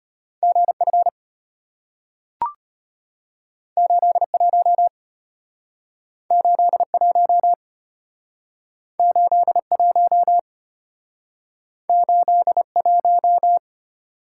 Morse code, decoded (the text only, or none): GF 81 81 81 81